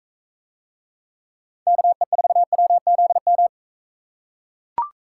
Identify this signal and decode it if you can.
Morse code, heard as a keyed tone, about 44 wpm, 700 Hz. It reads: KE4WZM